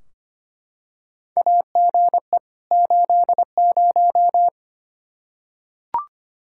Morse code, heard as a keyed tone, about 25 wpm, 700 Hz. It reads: AGE 80
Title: Practice QSO - Exchange 3 - More Info - 25wpm